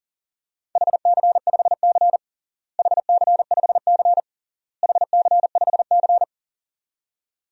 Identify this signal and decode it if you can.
Morse code, heard as a keyed tone, about 40 wpm, 700 Hz. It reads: HC5C HC5C HC5C
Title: DXCC-100 Call Signs - 4 Characters - Repeated 3x - 40wpm